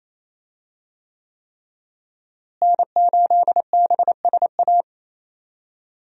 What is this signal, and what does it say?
Morse code, heard as a keyed tone, about 28 wpm, 700 Hz. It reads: N8BSA
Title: USA Call Signs - 5 Characters - Repeated 3x - 28wpm